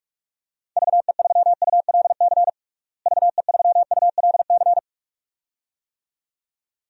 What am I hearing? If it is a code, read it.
Morse code, heard as a keyed tone, about 42 wpm, 700 Hz. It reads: VE3ULC VE3ULC
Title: DXCC-100 Call Signs - 6 Characters - Repeated 2x - 45wpm